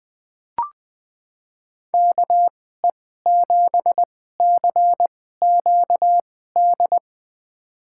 Morse code, heard as a keyed tone, about 20 wpm, spaced as for 14 wpm, 700 Hz. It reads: KE7CQD
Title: USA Call Signs - 6 Characters - 2x Character Spacing - 20wpm